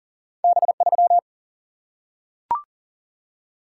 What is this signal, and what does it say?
Morse code, heard as a keyed tone, about 40 wpm, 700 Hz. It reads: B3
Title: Single-Letter Single-Number 40wpm